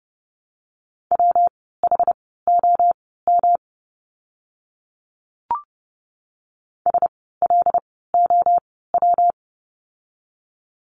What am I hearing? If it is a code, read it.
Morse code, heard as a keyed tone, about 30 wpm, spaced as for 17 wpm, 700 Hz. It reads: WHOM SLOW
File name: course / 30:10wpm / 115 Lesson 19f - Letter w - Words 30wpm